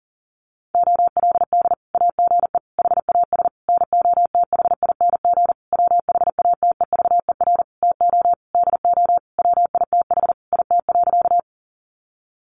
Morse code, heard as a keyed tone, about 40 wpm, 700 Hz. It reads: OLD AGE HAS NOTHING WHATEVER TO DO WITH IT.
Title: Sentences from Top 1100 Words 40wpm